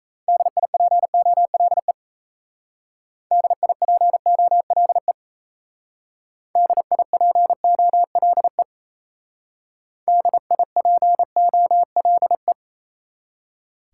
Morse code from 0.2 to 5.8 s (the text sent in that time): DIPOLE DIPOLE